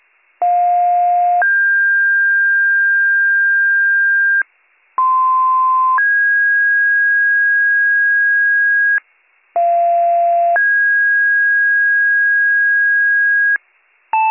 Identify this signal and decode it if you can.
Two-tone page, heard as a one-second tone, then a three-second tone, about 710 then 1600 Hz, about 1040 then 1650 Hz, about 690 then 1650 Hz. It is heard from 0.4 to 13.6 s.